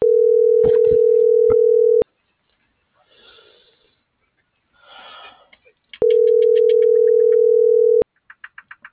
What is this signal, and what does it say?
Ringback tone (North America), 440 + 480 Hz.